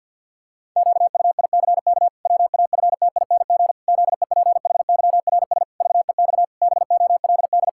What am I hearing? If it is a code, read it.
Morse code, heard as a keyed tone, about 50 wpm, 700 Hz. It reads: QUICK WAFTING ZEPHYRS VEX BOLD
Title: Pangram Challenge v2 - 50wpm